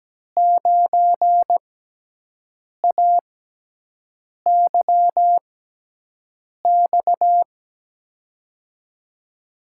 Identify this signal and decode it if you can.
Morse code, heard as a keyed tone, about 17 wpm, spaced as for 6 wpm, 700 Hz. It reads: 9AYX
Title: USA Call Signs - 6 Characters - 6x Character Spacing - 17wpm